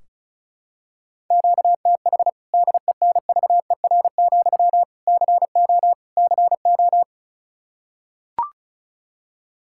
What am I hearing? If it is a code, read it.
Morse code, heard as a keyed tone, about 36 wpm, 700 Hz. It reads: QTH DENVER, CO CO
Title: K1USN SST QSO Practice 35wpm